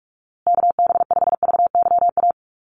Morse code, heard as a keyed tone, about 44 wpm, 700 Hz. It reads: KB5VYA
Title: USA Call Signs Repeated 2x 45wpm